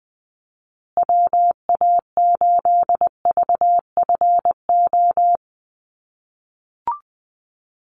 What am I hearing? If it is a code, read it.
Morse code, heard as a keyed tone, about 20 wpm, 700 Hz. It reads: WA8VFO
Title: USA Call Signs Repeated 3x 20wpm